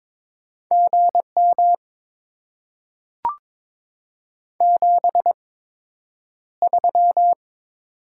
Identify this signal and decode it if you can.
Morse code, heard as a keyed tone, about 22 wpm, spaced as for 9 wpm, 700 Hz. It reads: GM 73
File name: Top 90 QSO Elements - 8x Character Spacing - 22wpm